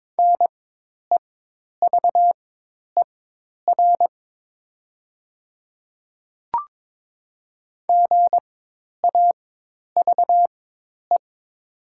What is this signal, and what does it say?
Morse code, heard as a keyed tone, about 22 wpm, spaced as for 10 wpm, 700 Hz. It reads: NEVER GAVE
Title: 4x Character Spacing - Top 500 Words - 22wpm